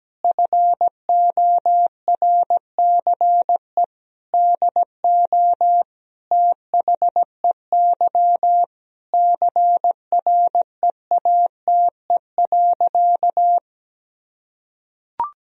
Morse code, heard as a keyed tone, about 17 wpm, 700 Hz. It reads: FORCE DO THEY CREATE.
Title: Sentences from Top 1000 Words - Review Part 3 - 17wpm